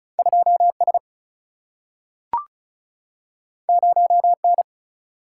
Morse code, heard as a keyed tone, about 36 wpm, 700 Hz. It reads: 2 S 0 N
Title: Single-Letter Single-Number 35wpm